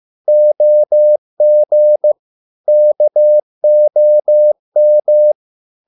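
Morse code, heard as a keyed tone, about 15 wpm, 600 Hz. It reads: OG KOM
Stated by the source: Morse code at a slow 15 wpm from CW med Gnister.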